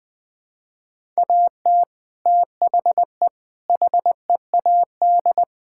Morse code, heard as a keyed tone, about 20 wpm, 700 Hz. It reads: AT THE HEAD
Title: Build A Phrase - 5-Word Phrases - 20wpm